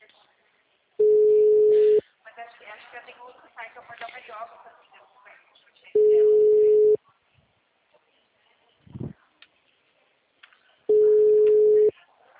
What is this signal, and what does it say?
Ringback tone (Europe, 425 Hz), with speech.
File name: out- -2012- - - .2991